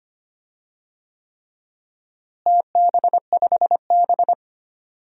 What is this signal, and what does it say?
Morse code, heard as a keyed tone, about 25 wpm, 700 Hz. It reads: TB5B
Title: DXCC-100 Call Signs - 4 Characters - Repeated 3x - 25wpm